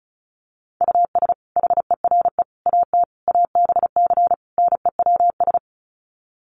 Morse code, heard as a keyed tone, about 34 wpm, 700 Hz. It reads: US HERE AT ABC NEWS
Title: Sets of 5 Words 35wpm